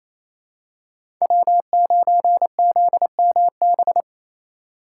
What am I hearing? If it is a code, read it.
Morse code, heard as a keyed tone, about 28 wpm, 700 Hz. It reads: W9ZMB
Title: USA Call Signs - 5 Characters - 2x Character Spacing - 28wpm